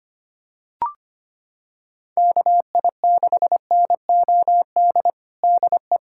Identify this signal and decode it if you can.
Morse code, heard as a keyed tone, about 25 wpm, 700 Hz. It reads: KI6NOD DE
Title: USA Callsign DE Callsign - 6 Characters - 25wpm